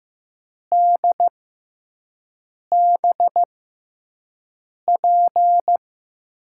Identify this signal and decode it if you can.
Morse code, heard as a keyed tone, about 15 wpm, spaced as for 5 wpm, 700 Hz. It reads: DBP
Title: USA Call Signs - 6 Characters - 6x Character Spacing - 15wpm